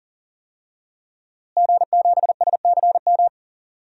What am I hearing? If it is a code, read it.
Morse code, heard as a keyed tone, about 40 wpm, 700 Hz. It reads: G7SCM
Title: DXCC-100 Call Signs - 5 Characters - Repeated 2x - 40wpm